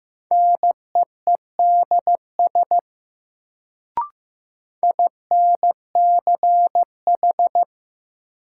Morse code, heard as a keyed tone, about 15 wpm, 700 Hz. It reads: NEEDS INCH
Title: Sentences from Top 1000 Words - Review Part 3 - New Word Drill - 15wpm